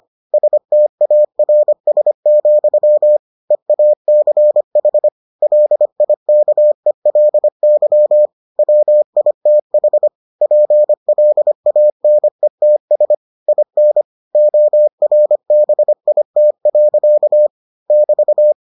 Morse code, heard as a keyed tone, about 25 wpm, 600 Hz. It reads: STARS, EACH LIKELY WITH PLANETS IN ORBIT. <BT>